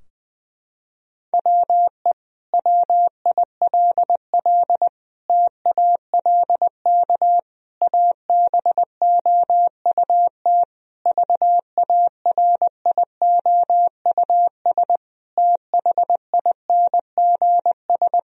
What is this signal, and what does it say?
Morse code, heard as a keyed tone, about 20 wpm, 700 Hz. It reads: WE WILL TALK ABOUT VARIOUS THINGS
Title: Sentences from Top 1000 Words - Review Part 4 - 20wpm